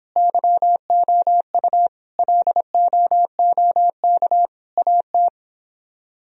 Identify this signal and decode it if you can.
Morse code, heard as a keyed tone, about 29 wpm, 700 Hz. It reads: YOU LOOK AT